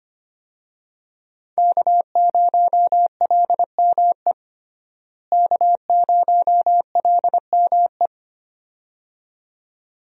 Morse code, heard as a keyed tone, about 25 wpm, 700 Hz. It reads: K0LME K0LME
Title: USA Call Signs - 5 Characters - Repeated 2x - 25wpm